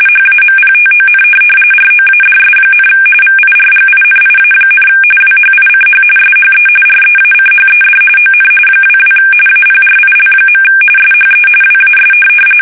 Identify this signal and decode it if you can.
RTTY, about 45.45 baud. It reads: HAMPTON ROADS, VIRGINIA, WHICH SHE REACHED ON 22 APRIL. SHE WAS
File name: fsk850 / USS-Massachusetts